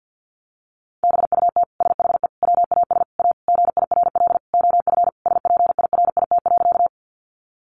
Morse code, heard as a keyed tone, about 50 wpm, 700 Hz. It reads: BUT SHE WAS A GIRL OF SPIRIT.
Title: Sentences from Top 1400 Words 50wpm